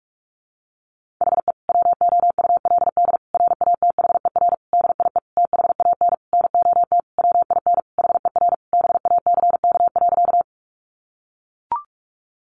Morse code, heard as a keyed tone, about 44 wpm, 700 Hz. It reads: HE WOULD RATHER DIE THAN NOT WIN HER BACK.
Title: Sentences from Top 1500 Words 45wpm